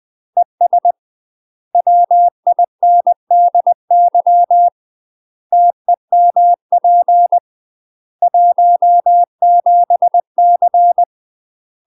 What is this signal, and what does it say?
Morse code, heard as a keyed tone, about 20 wpm, 700 Hz. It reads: ES WINDY TEMP 17C
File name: UK MASTER QSOs TRAINING / 20WPM / UK MASTER QSOs 0038